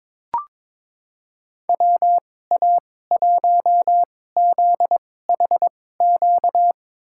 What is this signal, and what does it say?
Morse code, heard as a keyed tone, about 22 wpm, spaced as for 16 wpm, 700 Hz. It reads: WA1ZHQ
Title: USA Call Signs - 6 Characters - 2x Character Spacing - 22wpm